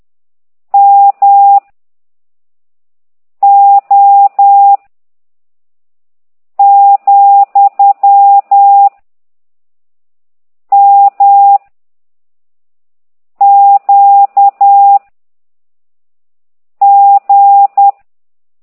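Morse code, heard as a keyed tone, about 10 wpm, 800 Hz. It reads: M O , M Q G
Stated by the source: Morse code lesson 5 exercise 10 by CT2IDW, containing only characters M, O, G, Z, Q and comma.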